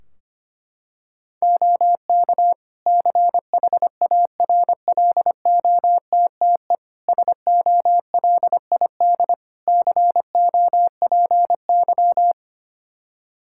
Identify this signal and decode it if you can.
Morse code, heard as a keyed tone, about 25 wpm, 700 Hz. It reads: OK CHARLOTTE SOLID COPY
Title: Practice QSO - Exchange 3 - More Info - 25wpm